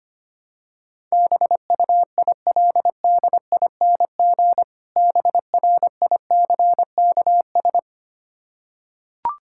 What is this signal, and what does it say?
Morse code, heard as a keyed tone, about 25 wpm, 690 Hz. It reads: BUILDING BRICKS
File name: practice / 25wpm / Difficult Words with Hints - 25wpm